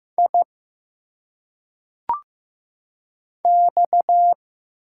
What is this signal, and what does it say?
Morse code, heard as a keyed tone, about 15 wpm, 700 Hz. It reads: I X
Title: Single Letters - Phonetic Pronunciation 15wpm